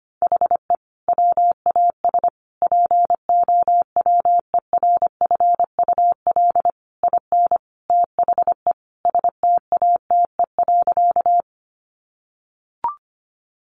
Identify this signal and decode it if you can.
Morse code, heard as a keyed tone, about 25 wpm, 700 Hz. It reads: HE WAS POWERFUL IN THE STATE.
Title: Sentences from Top 1300 Words 25wpm